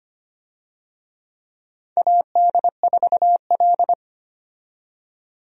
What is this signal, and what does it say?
Morse code, heard as a keyed tone, about 25 wpm, 700 Hz. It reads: AD4L
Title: USA Call Signs - 4 Characters - 25wpm